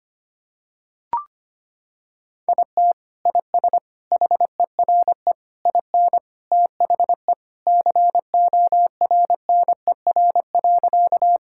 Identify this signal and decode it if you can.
Morse code, heard as a keyed tone, about 25 wpm, 700 Hz. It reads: IT IS HERE IN THE CORNER.
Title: Sentences from Top 1600 Words 25wpm